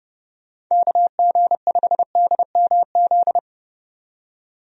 Morse code, heard as a keyed tone, about 30 wpm, 700 Hz. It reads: KG5DMZ
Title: USA Call Signs - 6 Characters - 30wpm